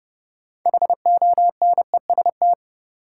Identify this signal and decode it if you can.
Morse code, heard as a keyed tone, about 32 wpm, 700 Hz. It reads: HONEST
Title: Sets of 5 Words Encore - New Word Drill - 30wpm